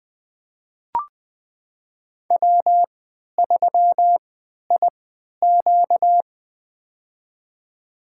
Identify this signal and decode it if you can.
Morse code, heard as a keyed tone, about 20 wpm, spaced as for 11 wpm, 700 Hz. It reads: W3IQ
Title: USA Call Signs - 4 Characters - 3x Character Spacing - 20wpm